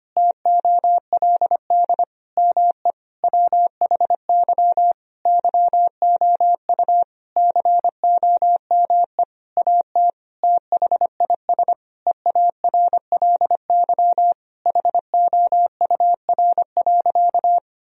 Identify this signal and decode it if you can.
Morse code, heard as a keyed tone, about 25 wpm, 700 Hz. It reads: TOLD ME WHY YOU COME AT THIS EARLY HOUR.
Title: Sentences from Top 1000 Words - Review Part 2 - 25wpm